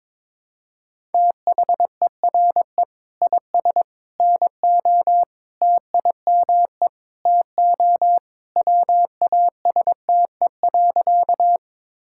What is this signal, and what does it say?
Morse code, heard as a keyed tone, about 22 wpm, 700 Hz. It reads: THERE IS NO TIME TO WASTE.
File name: practice / 22wpm / Sentences with 3-Word Phrases - 22wpm